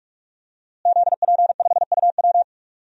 Morse code, heard as a keyed tone, about 44 wpm, 700 Hz. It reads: ZP5UW
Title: DXCC-100 Call Signs - 5 Characters - Repeated 2x - 45wpm